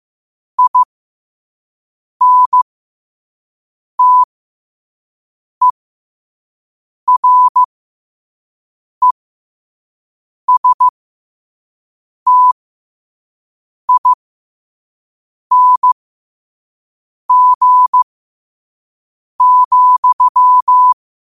Morse code, heard as a keyed tone, about 15 wpm, spaced as for 5 wpm, 1000 Hz. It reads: INTERESTING,
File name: qotdfiles / 5wpm